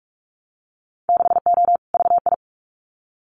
Morse code, heard as a keyed tone, about 46 wpm, 700 Hz. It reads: 6O VI
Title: Winter Field Day Exchanges Encore 45wpm